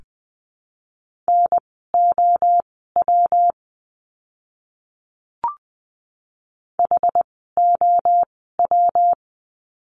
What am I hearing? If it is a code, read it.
Morse code, heard as a keyed tone, about 20 wpm, spaced as for 15 wpm, 700 Hz. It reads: NOW HOW
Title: 146 Lesson 24f - Letter y - Words 20wpm